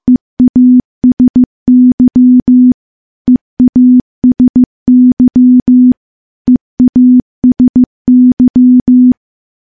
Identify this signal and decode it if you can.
Morse code, heard as a keyed tone, about 15 wpm, 250 Hz. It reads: EASY EASY EASY
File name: morse / morse-training-15WPM